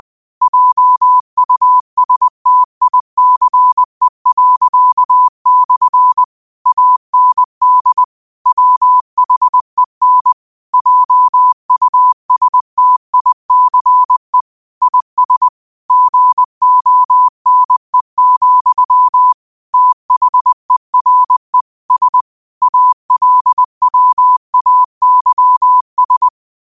Morse code, heard as a keyed tone, about 20 wpm, 1000 Hz. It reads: JUSTICE./ AND WHEN JUSTICE IS GONE, THERE S ALWAYS